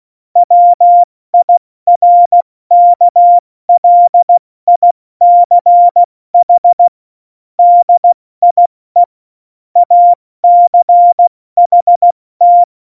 Morse code, heard as a keyed tone, about 16 wpm, 700 Hz. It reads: WIRKLICH DIE ACHT
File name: Morsekurs / Witze / Witz 27. Februar 2018, 70 F80